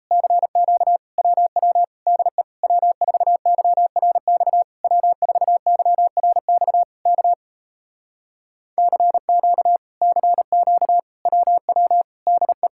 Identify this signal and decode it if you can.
Morse code, heard as a keyed tone, about 38 wpm, 700 Hz. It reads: CQ WW DE W4YRX W4YRX K CQ CQ WW DE